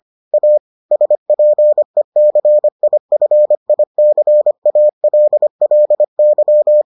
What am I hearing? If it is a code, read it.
Morse code, heard as a keyed tone, about 25 wpm, 600 Hz. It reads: A SPECIFICALLY